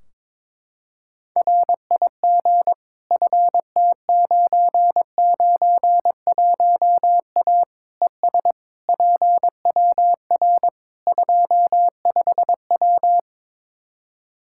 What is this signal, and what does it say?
Morse code, heard as a keyed tone, about 22 wpm, 700 Hz. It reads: RIG FT991A ES PWR 25W
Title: Practice QSO - Exchange 2 - Info - 22wpm